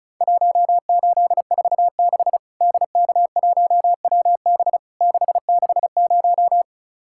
Morse code, heard as a keyed tone, about 36 wpm, 700 Hz. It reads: 1846 DK1WB 660
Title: Worked All Europe DX Contest - QTC Exchange - 35wpm